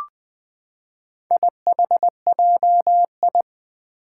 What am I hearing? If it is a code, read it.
Morse code, heard as a keyed tone, about 20 wpm, 700 Hz. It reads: IHJI
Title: Four Challenging Letter-Number Combinations 20wpm